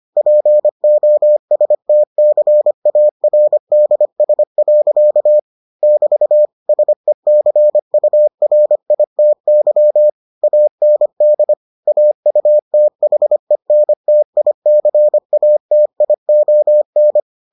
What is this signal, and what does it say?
Morse code, heard as a keyed tone, about 25 wpm, 600 Hz. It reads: POSTCARDS. <BT> SECURITY AND AUTHENTICATION